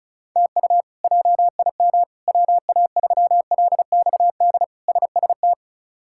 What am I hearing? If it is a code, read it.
Morse code, heard as a keyed tone, about 36 wpm, 700 Hz. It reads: TU JIM WA3LXD SST
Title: K1USN SST QSO Practice - 35wpm